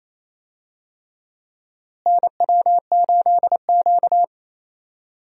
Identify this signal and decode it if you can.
Morse code, heard as a keyed tone, about 28 wpm, 700 Hz. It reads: NW8Q
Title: USA Call Signs - 4 Characters - Repeated 3x - 28wpm